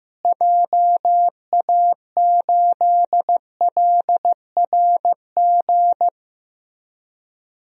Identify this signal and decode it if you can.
Morse code, heard as a keyed tone, about 15 wpm, 700 Hz. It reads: JA8LRG
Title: DXCC-100 Call Signs - 6 Characters - Repeated 3x - 15wpm